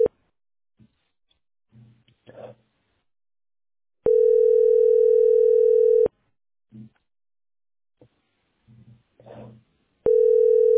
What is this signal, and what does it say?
Ringback tone (North America), 440 + 480 Hz.